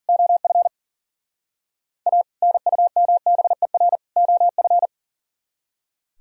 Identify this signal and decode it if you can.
Morse code, heard as a keyed tone, about 38 wpm, 700 Hz. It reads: OF A NUMBER OF
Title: Speed-Racing - Sets of 3 Words 40wpm